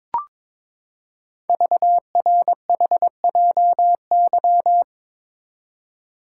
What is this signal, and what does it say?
Morse code, heard as a keyed tone, about 22 wpm, 700 Hz. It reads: VRHJY
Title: Code Groups - 5 Letter:Numbers - 22wpm